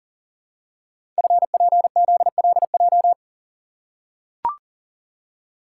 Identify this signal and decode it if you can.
Morse code, heard as a keyed tone, about 39 wpm, 700 Hz. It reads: FPZLJ
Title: Code Groups - 5 Letter:Numbers - 40wpm